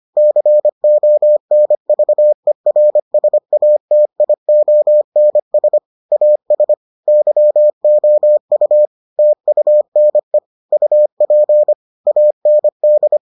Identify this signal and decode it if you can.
Morse code, heard as a keyed tone, about 25 wpm, 600 Hz. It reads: CONVERSATIONS AS YOU TUNE UP AND